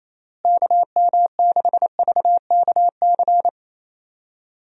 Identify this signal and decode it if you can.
Morse code, heard as a keyed tone, about 27 wpm, 700 Hz. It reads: KM6VKC
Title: USA Call Signs - 6 Characters - Repeated 2x - 28wpm